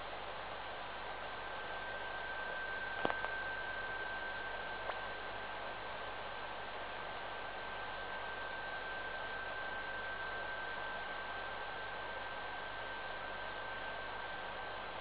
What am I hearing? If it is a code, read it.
PSK31, about 31.25 baud.